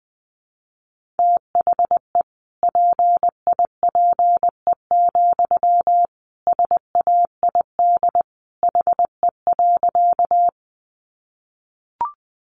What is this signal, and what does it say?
Morse code, heard as a keyed tone, about 20 wpm, 700 Hz. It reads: THE PIPE, SAID HE.
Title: Sentences from Top 1500 Words 20wpm